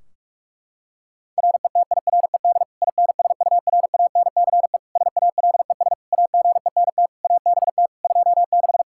Morse code, heard as a keyed tone, about 46 wpm, 700 Hz. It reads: RETIRED INSURANCE SALES AGENT ABT 26
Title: Practice QSO - Exchange 3 - More Info - 45wpm